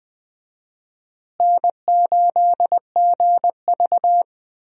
Morse code, heard as a keyed tone, about 20 wpm, 700 Hz. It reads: N8GV
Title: USA Call Signs - 4 Characters - Repeated 2x - 20wpm